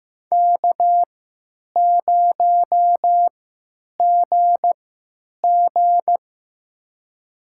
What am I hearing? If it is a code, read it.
Morse code, heard as a keyed tone, about 15 wpm, spaced as for 9 wpm, 700 Hz. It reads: K0GG
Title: USA Call Signs - 4 Characters - 3x Character Spacing - 15wpm